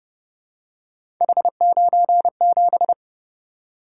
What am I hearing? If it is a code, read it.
Morse code, heard as a keyed tone, about 29 wpm, 700 Hz. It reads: H97